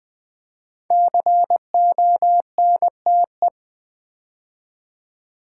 Morse code, heard as a keyed tone, about 20 wpm, 700 Hz. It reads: CONTE
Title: Build A Word - 5 Letters At A Time - 12-Letter Words - 20wpm